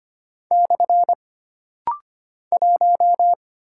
Morse code, heard as a keyed tone, about 25 wpm, 700 Hz. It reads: / 1
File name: Single Character (Letter:Number:Punctuation:Prosign) - Rapid-Fire - 25wpm